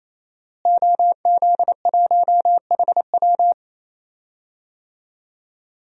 Morse code, heard as a keyed tone, about 28 wpm, 700 Hz. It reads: OZ1HW
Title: DXCC-100 Call Signs - 5 Characters - Repeated 3x - 28wpm